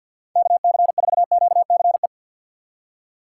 Morse code, heard as a keyed tone, about 48 wpm, 700 Hz. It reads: KC4QCE
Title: USA Call Signs - 6 Characters - 4x Character Spacing - 50wpm